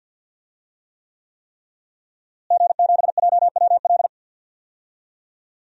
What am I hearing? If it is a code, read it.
Morse code, heard as a keyed tone, about 52 wpm, 700 Hz. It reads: G7JWL
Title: DXCC-100 Call Signs - 5 Characters - Repeated 3x - 50wpm